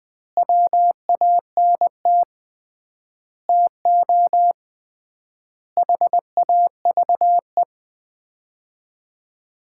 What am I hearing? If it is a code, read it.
Morse code, heard as a keyed tone, about 20 wpm, 700 Hz. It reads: WANT TO HAVE
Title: 3x Word-Spacing - Sets of 5 Words - 20wpm